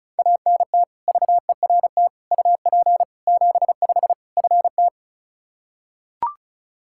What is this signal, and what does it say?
Morse code, heard as a keyed tone, about 36 wpm, 700 Hz. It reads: ANT VERT UP 75 FT
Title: K1USN SST QSO Practice 35wpm